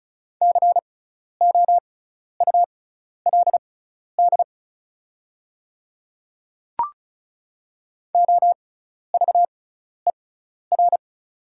Morse code, heard as a keyed tone, about 36 wpm, spaced as for 12 wpm, 700 Hz. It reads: COULD OVER
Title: 6x Character Spacing - Top 200 Words - 35wpm